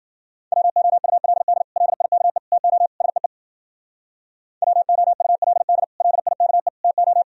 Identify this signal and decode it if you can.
Morse code, heard as a keyed tone, about 51 wpm, 700 Hz. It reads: WOULD LIKE TO SEE WOULD LIKE TO